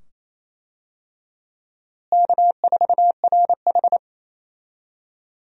Morse code, heard as a keyed tone, about 28 wpm, 700 Hz. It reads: K4RH
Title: USA Call Signs - 4 Characters - Repeated 2x - 28wpm